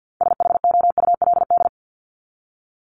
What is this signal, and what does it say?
Morse code, heard as a keyed tone, about 50 wpm, 700 Hz. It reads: SHOULD
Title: Top 300 Words 50wpm